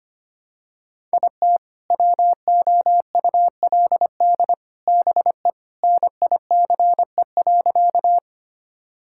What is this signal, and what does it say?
Morse code, heard as a keyed tone, about 25 wpm, 700 Hz. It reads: IT WOULD BE NICE.
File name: practice / 25wpm / Sentences with 3-Word Phrases - 25wpm